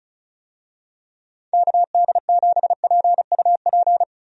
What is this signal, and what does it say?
Morse code, heard as a keyed tone, about 34 wpm, 700 Hz. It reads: KD7PUP
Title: USA Call Signs - 6 Characters - Repeated 2x - 35wpm